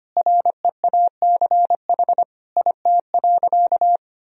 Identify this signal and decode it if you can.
Morse code, heard as a keyed tone, about 25 wpm, 700 Hz. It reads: REACH IT.
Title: Sentences from Top 400 Words and Abbreviations 25wpm